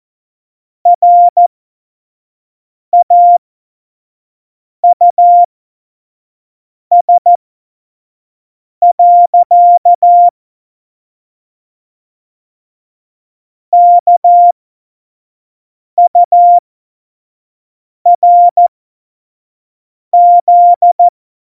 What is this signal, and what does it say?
Morse code, heard as a keyed tone, about 14 wpm, spaced as for 5 wpm, 700 Hz. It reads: RAUS. KURZ
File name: Morsekurs / Witze / Witz 23. Januar 2018, 25 F70